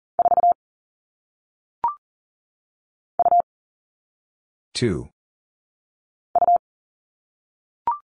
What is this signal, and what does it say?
Morse code, heard as a keyed tone, about 40 wpm, 700 Hz. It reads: 4 U U
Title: All Cut Numbers - 1-Digit 40wpm